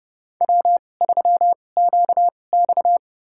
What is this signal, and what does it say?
Morse code, heard as a keyed tone, about 30 wpm, spaced as for 22 wpm, 700 Hz. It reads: W3QX
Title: USA Call Signs - 4 Characters - 2x Character Spacing - 30wpm